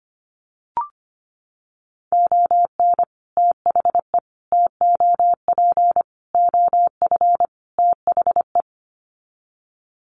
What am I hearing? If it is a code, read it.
Morse code, heard as a keyed tone, about 25 wpm, 700 Hz. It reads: ON THE TOP OF THE
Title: Sets of 5 Words Encore 25wpm